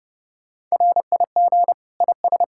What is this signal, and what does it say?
Morse code, heard as a keyed tone, about 34 wpm, 700 Hz. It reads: RIG IS
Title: Speed-Racing Common QSO Phrases Encore - 30wpm